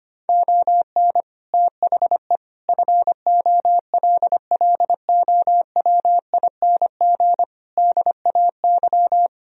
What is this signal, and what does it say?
Morse code, heard as a keyed tone, about 25 wpm, 700 Hz. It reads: ON THE FOLLOWING DAY
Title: Sentences from Top 1000 Words - Review Part 4 - 25wpm